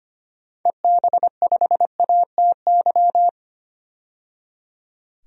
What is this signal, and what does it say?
Morse code, heard as a keyed tone, about 25 wpm, 700 Hz. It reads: EB5ATY
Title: DXCC-100 Call Signs - 6 Characters - 25wpm